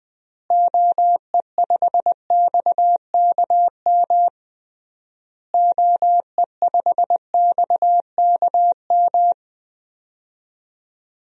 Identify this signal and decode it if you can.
Morse code, heard as a keyed tone, about 21 wpm, 700 Hz. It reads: OE5XKM OE5XKM
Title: DXCC-100 Call Signs - 6 Characters - Repeated 3x - 20wpm